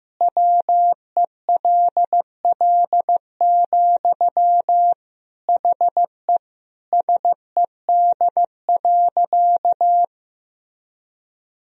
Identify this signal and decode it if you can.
Morse code, heard as a keyed tone, about 15 wpm, 700 Hz. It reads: WELL, HE SED.
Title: Sentences from Top 500 Words and Abbreviations 15wpm